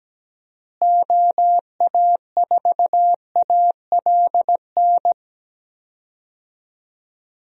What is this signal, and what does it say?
Morse code, heard as a keyed tone, about 17 wpm, 700 Hz. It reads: OA4ALN
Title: USA Call Signs - 6 Characters - Repeated 2x - 17wpm